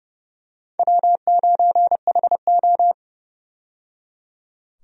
Morse code, heard as a keyed tone, about 29 wpm, 700 Hz. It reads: W9HO
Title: USA Call Signs - 4 Characters - 30wpm